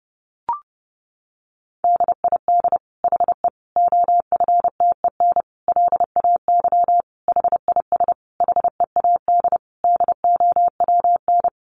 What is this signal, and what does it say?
Morse code, heard as a keyed tone, about 30 wpm, 700 Hz. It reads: DID HE OFTEN LAY HIS HEAD DOWN
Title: Sentences from Top 1000 Words 30wpm